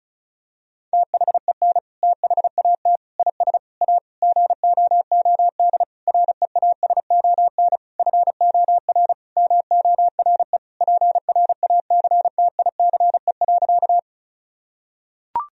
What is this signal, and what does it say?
Morse code, heard as a keyed tone, about 34 wpm, 700 Hz. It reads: THEN THAT IS A GOOD REASON FOR MORE PRACTICE.
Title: Sentences from Top 1000 Words - Review Part 4 - 35wpm